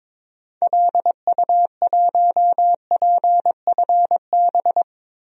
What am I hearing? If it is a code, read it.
Morse code, heard as a keyed tone, about 22 wpm, 700 Hz. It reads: LU1PFB
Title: DXCC-100 Call Signs - 6 Characters - Repeated 2x - 22wpm